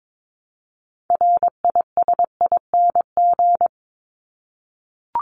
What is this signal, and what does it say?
Morse code, heard as a keyed tone, about 22 wpm, 700 Hz. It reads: RISING
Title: ING Suffix - Top 100 Words with Base Words 22wpm